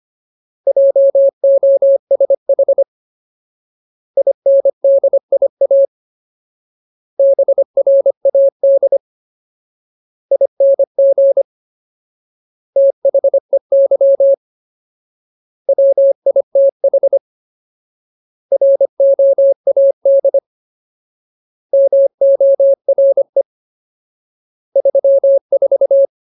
Morse code, heard as a keyed tone, about 25 wpm, 550 Hz. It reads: JOSH INDIA BRAD ING THEY WITH ROAD MORE 34
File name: Ultimate Word List / 25WPM Ultimate